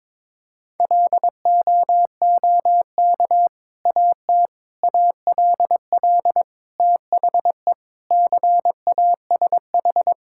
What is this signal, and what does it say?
Morse code, heard as a keyed tone, about 22 wpm, 700 Hz. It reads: LOOK AT ALL THE CASH
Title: Sentences from Top 1000 Words - Review Part 5 - 22wpm